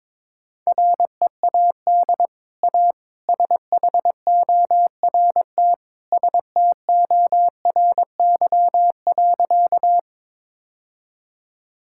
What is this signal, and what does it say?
Morse code, heard as a keyed tone, about 22 wpm, 700 Hz. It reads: READ A SHORT STORY.
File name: practice / 22wpm / Easy Sentences with Hints - 22wpm